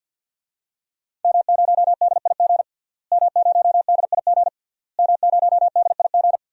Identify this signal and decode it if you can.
Morse code, heard as a keyed tone, about 52 wpm, 700 Hz. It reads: M0DIG M0DIG M0DIG
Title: DXCC-100 Call Signs - 5 Characters - Repeated 3x - 50wpm